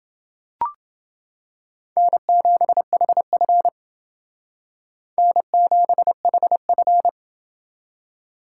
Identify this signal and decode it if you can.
Morse code, heard as a keyed tone, about 29 wpm, 700 Hz. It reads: N7HF N7HF